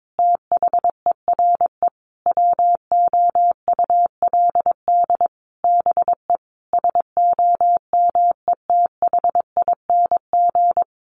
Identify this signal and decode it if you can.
Morse code, heard as a keyed tone, about 22 wpm, 700 Hz. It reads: THERE WOULD BE SOMETHING
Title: Sentences from Top 1200 Words 22wpm